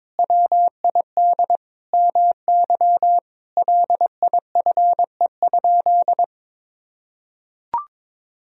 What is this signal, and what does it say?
Morse code, heard as a keyed tone, about 22 wpm, 700 Hz. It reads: WID MY LIFE?
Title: Sentences from Top 300 Words and Abbreviations 22wpm